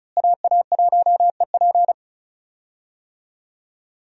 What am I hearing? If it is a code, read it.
Morse code, heard as a keyed tone, about 36 wpm, 700 Hz. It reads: AA1EP
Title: USA Call Signs - 5 Characters - Repeated 2x - 35wpm